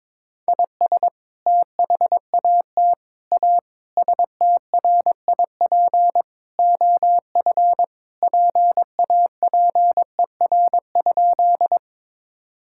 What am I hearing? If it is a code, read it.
Morse code, heard as a keyed tone, about 22 wpm, 700 Hz. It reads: IS THAT A STRIP OF PAPER?
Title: Sentences from Top 1600 Words 22wpm